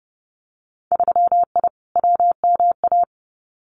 Morse code, heard as a keyed tone, about 30 wpm, 700 Hz. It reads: 3I WMA
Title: Winter Field Day Exchanges Encore 30wpm